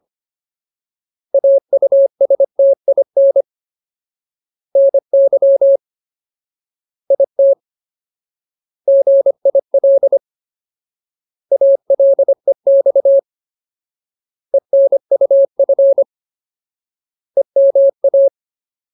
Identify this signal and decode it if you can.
Morse code, heard as a keyed tone, about 25 wpm, 550 Hz. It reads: AUSTIN NY IT GIL ALEX ENUF EMA
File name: Ultimate Word List / 25WPM Ultimate